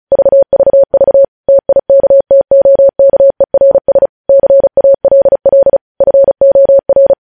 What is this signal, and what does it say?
Morse code, heard as a keyed tone, about 36 wpm, 550 Hz. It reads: VVV TIKTOKERS CALL FOR
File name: podcast / News headlines-1x-35x35